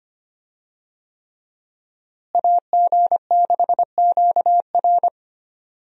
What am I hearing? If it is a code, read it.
Morse code, heard as a keyed tone, about 25 wpm, 700 Hz. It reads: AG6QR
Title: USA Call Signs - 5 Characters - Repeated 3x - 25wpm